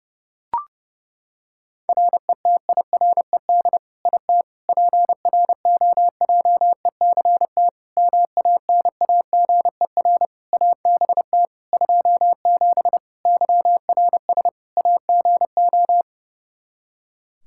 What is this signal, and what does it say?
Morse code, heard as a keyed tone, about 32 wpm, 700 Hz. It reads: RETIRED IT PROJECT MANAGER ABT 27 YRS AGO
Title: Practice QSO - Exchange 3 - More Info - 30wpm